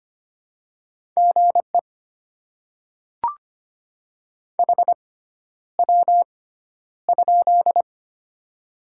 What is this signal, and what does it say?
Morse code, heard as a keyed tone, about 25 wpm, spaced as for 11 wpm, 700 Hz. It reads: GE HW?
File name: Top 20 QSO Elements - 6x Character Spacing - 25wpm